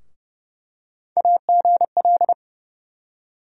Morse code, heard as a keyed tone, about 30 wpm, 700 Hz. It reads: AGL
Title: USA Call Sign Postfixes - 3 Letters 30wpm